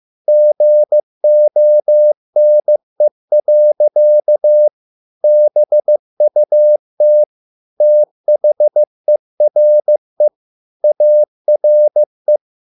Morse code, heard as a keyed tone, about 15 wpm, 600 Hz. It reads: GONE. BUT THERE ARE